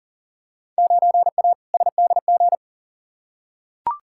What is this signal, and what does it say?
Morse code, heard as a keyed tone, about 38 wpm, 700 Hz. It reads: 9A SDG
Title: Field Day Exchanges Encore - 40wpm